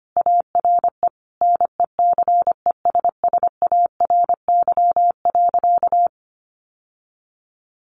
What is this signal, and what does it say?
Morse code, heard as a keyed tone, about 25 wpm, 700 Hz. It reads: ARE NECESSARY.
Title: Sentences from Top 1400 Words 25wpm